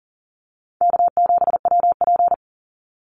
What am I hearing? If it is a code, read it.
Morse code, heard as a keyed tone, about 37 wpm, 700 Hz. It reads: K7WP
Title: Easy Call Signs Repeated 2x - 40wpm